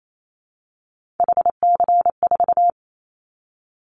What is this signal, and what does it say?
Morse code, heard as a keyed tone, about 28 wpm, 700 Hz. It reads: HC4
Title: Callsign Prefixes for Top 100 DXCC - 28wpm